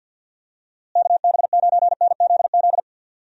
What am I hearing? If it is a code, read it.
Morse code, heard as a keyed tone, about 50 wpm, 700 Hz. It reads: KB9NZZ
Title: USA Call Signs - 6 Characters - Repeated 2x - 50wpm